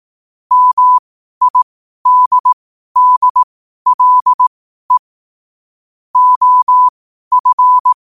Morse code, heard as a keyed tone, about 18 wpm, spaced as for 12 wpm, 1000 Hz. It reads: MIDDLE OF
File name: qotdfiles / 12wpm